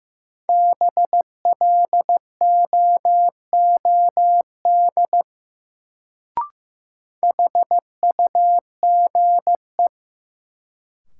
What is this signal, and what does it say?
Morse code, heard as a keyed tone, about 15 wpm, 700 Hz. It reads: BLOOD HUGE
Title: Sentences from Top 1100 Words - New Word Drill Alt - 15wpm